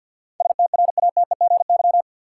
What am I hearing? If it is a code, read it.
Morse code, heard as a keyed tone, about 51 wpm, 700 Hz. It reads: STRATEGY
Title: Sentences from Top 1000 Words - Review Part 5 - New Word Drill - 50wpm